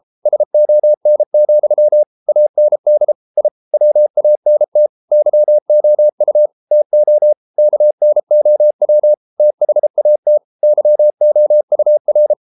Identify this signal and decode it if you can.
Morse code, heard as a keyed tone, about 33 wpm, 600 Hz. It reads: SON, AND I WANT YOU TO KNOW THAT YOUR